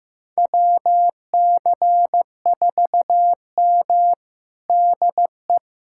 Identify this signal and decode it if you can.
Morse code, heard as a keyed tone, about 15 wpm, 700 Hz. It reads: WC4M DE
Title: USA Callsign DE Callsign 15wpm